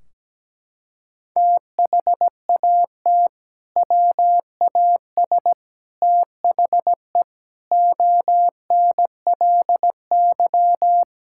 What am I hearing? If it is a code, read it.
Morse code, heard as a keyed tone, about 17 wpm, 700 Hz. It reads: THAT WAS THE ONLY
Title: Sentences from Top 1600 Words 17wpm